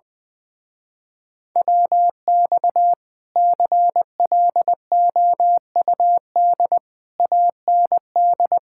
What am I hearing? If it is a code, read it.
Morse code, heard as a keyed tone, about 21 wpm, 700 Hz. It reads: WX CLOUD AND
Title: Speed-Racing Common QSO Phrases - 20wpm